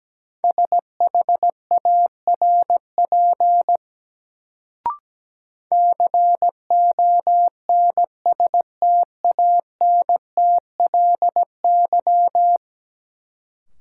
Morse code, heard as a keyed tone, about 17 wpm, 700 Hz. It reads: SHARP CONSTANTLY
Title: Sentences from Top 1600 Words - New Word Drill Alt - 17wpm